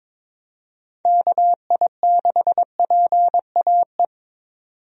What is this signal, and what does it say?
Morse code, heard as a keyed tone, about 22 wpm, 700 Hz. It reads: KI6PAE
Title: USA Call Signs - 6 Characters - 22wpm